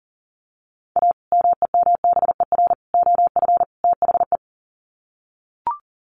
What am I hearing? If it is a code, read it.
Morse code, heard as a keyed tone, about 40 wpm, 700 Hz. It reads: A MEMBER OF THE
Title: Sets of 4 Words 40wpm